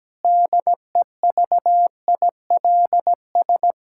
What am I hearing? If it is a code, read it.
Morse code, heard as a keyed tone, about 17 wpm, 700 Hz. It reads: DEVILS
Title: Pangram Challenge v2 - 17wpm